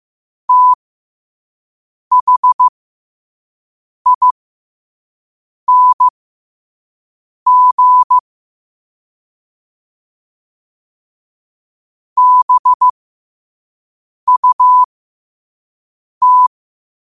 Morse code, heard as a keyed tone, about 15 wpm, spaced as for 5 wpm, 1000 Hz. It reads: THING BUT
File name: qotdfiles / 5wpm